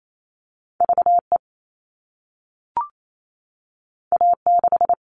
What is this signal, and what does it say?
Morse code, heard as a keyed tone, about 28 wpm, 700 Hz. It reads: VE A6
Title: All Cut Numbers - 2-Digit 28wpm